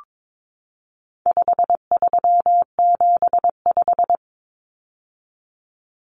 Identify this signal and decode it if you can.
Morse code, heard as a keyed tone, about 22 wpm, 700 Hz. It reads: 5375
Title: Common Cut Numbers - 4-Digit 22wpm